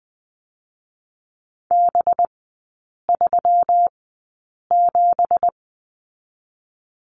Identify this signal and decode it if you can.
Morse code, heard as a keyed tone, about 20 wpm, spaced as for 8 wpm, 700 Hz. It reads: B37
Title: 205 Lesson 33e - Letter j - Three Characters 20wpm